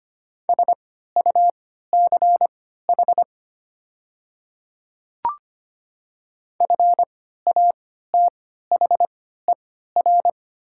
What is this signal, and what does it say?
Morse code, heard as a keyed tone, about 25 wpm, spaced as for 14 wpm, 700 Hz. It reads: SUCH FATHER
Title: 3x Character Spacing - Top 200 Words - 25wpm